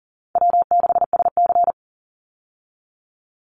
Morse code, heard as a keyed tone, about 40 wpm, 700 Hz. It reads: W6SC
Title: Call Signs Repeated 3x - 40wpm